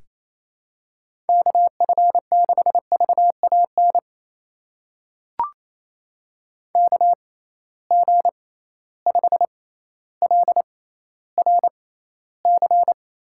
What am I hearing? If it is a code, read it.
Morse code, heard as a keyed tone, about 28 wpm, spaced as for 14 wpm, 700 Hz. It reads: KF6VAN KG5LRC
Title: USA Call Signs - 6 Characters - 6x Character Spacing - 28wpm